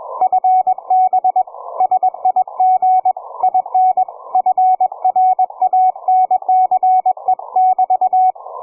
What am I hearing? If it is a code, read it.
Morse code, heard as a keyed tone, about 21 wpm, 740 Hz, with radio noise. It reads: FB SIG IN FRANCE<BT>
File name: cw forum / ve1zj 1830 160m 04h00